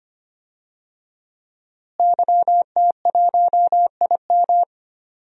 Morse code, heard as a keyed tone, about 25 wpm, 700 Hz. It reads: YT1IM
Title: DXCC-100 Call Signs - 5 Characters - Repeated 3x - 25wpm